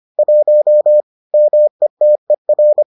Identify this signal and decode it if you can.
Morse code, heard as a keyed tone, about 25 wpm, 600 Hz. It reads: 1 METER